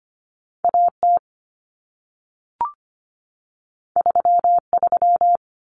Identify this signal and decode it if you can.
Morse code, heard as a keyed tone, about 25 wpm, 700 Hz. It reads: AT 33
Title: Common Cut Numbers - 2-Digit 25wpm